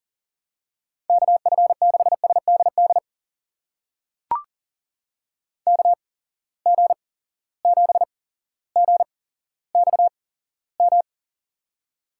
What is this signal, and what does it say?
Morse code, heard as a keyed tone, about 37 wpm, spaced as for 17 wpm, 700 Hz. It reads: KF6SDD KG7GXM
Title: USA Call Signs - 6 Characters - 8x Character Spacing - 40wpm